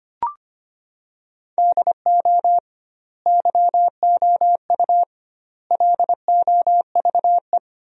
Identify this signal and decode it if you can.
Morse code, heard as a keyed tone, about 25 wpm, 700 Hz. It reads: DO YOU LOVE
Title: 2x Word-Spacing - Sentences from Top 400 Words - 25wpm